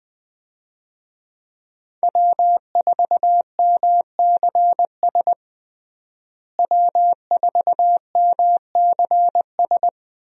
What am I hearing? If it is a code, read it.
Morse code, heard as a keyed tone, about 20 wpm, 700 Hz. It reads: W4MCS W4MCS
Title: USA Call Signs - 5 Characters - Repeated 3x - 20wpm